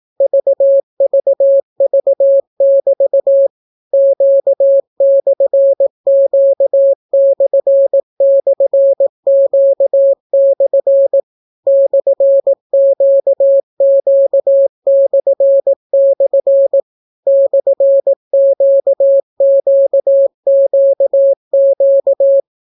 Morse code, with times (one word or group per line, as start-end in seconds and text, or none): 0.2-3.5 VVV<BT>
3.9-11.2 Q/Q//Q/
11.7-16.8 /QQ//
17.3-22.4 /QQQQ